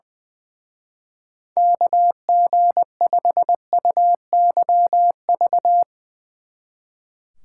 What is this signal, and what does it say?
Morse code, heard as a keyed tone, about 20 wpm, 700 Hz. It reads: KG5UYV
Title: Speed Racing - USA 2x3 Call Signs 20wpm